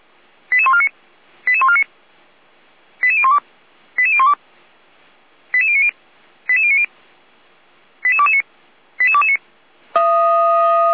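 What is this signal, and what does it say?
Five-tone selective call (ZVEI), heard as quick runs of five tones, tones of about 70 ms. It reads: 88149 88149 88012 88012 88090 88090 88229 88229
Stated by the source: Five-tone ZVEI selective call received by SDR, logged by the fire brigade (Feuerwehr) Orlamünde.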